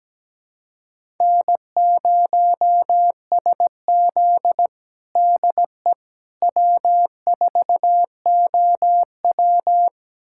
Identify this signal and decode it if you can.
Morse code, heard as a keyed tone, about 17 wpm, 700 Hz. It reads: N0SZ DE W4OW
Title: USA Callsign DE Callsign - 4 Characters - 17wpm